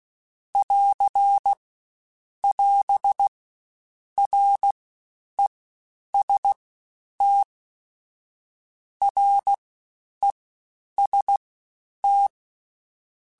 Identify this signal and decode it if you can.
Morse code, heard as a keyed tone, about 16 wpm, 800 Hz. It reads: <AR> <AS> R E S T R E S T
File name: Combinations / Combinations-mp3s / Combinations@15